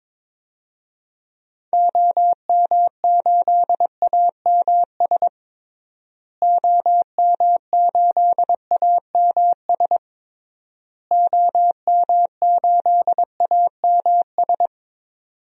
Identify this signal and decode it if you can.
Morse code, heard as a keyed tone, about 22 wpm, 700 Hz. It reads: OM8AMS OM8AMS OM8AMS
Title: DXCC-100 Call Signs - 6 Characters - Repeated 3x - 22wpm